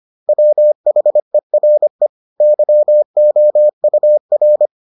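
Morse code, heard as a keyed tone, about 25 wpm, 600 Hz. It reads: WHERE YOUR
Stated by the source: Morse edition of the Foundations of Amateur Radio episodes.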